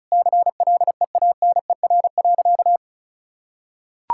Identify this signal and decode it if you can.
Morse code, heard as a keyed tone, about 34 wpm, 700 Hz. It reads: CLEANER.
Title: Pangram Challenge v2 - 35wpm